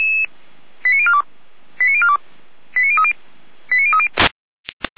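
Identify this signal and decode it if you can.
Five-tone selective call (ZVEI), heard as quick runs of five tones, tones of about 80 ms. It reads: 89042 89042 89030 89030
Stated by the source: ZVEI five-tone selective call received by SDR.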